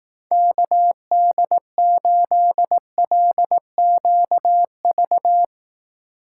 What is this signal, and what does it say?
Morse code, heard as a keyed tone, about 18 wpm, 700 Hz. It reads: KD8LQV
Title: Speed Racing - USA 2x3 Call Signs 15wpm